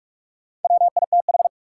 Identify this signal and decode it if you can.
Morse code, heard as a keyed tone, about 47 wpm, 700 Hz. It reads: WITH
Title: Sentences from Top 100 Words - New Word Drill - 45wpm